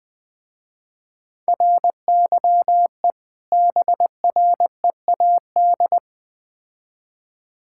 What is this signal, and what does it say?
Morse code, heard as a keyed tone, about 20 wpm, 700 Hz. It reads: RYE BREAD
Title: Difficult Words with Hints - 20wpm